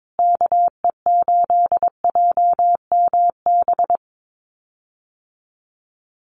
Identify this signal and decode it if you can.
Morse code, heard as a keyed tone, about 22 wpm, 700 Hz. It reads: KE8JMB
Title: USA Call Signs Repeated 3x 22wpm